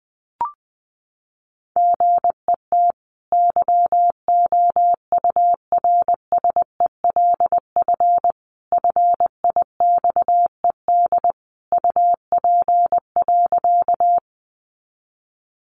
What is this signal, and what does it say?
Morse code, heard as a keyed tone, about 20 wpm, 700 Hz. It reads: GET YOURSELF FIXED UP.
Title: Sentences from Top 1100 Words 20wpm